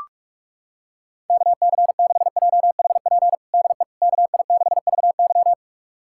Morse code, heard as a keyed tone, about 46 wpm, 700 Hz. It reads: KC6JHP DE KI6VY
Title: USA Callsign DE Callsign 45wpm